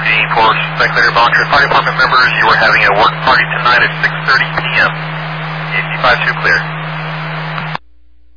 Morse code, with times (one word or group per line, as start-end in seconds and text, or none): none